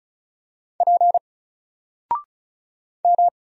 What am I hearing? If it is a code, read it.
Morse code, heard as a keyed tone, about 35 wpm, 700 Hz. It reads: P M
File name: Single-Letter-Number-35wpm